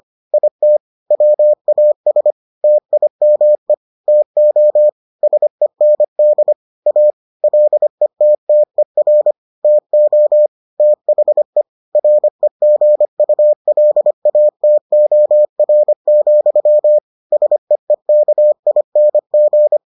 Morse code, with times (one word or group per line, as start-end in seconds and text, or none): 0.3-0.8 IT
1.1-2.3 WAS
2.6-3.7 TIME
4.1-4.9 TO
5.2-6.5 SEND
6.9-7.1 A
7.4-9.3 LETTER
9.6-10.5 TO
10.8-11.6 THE
11.9-17.0 REGULATOR,
17.3-19.8 SEEKING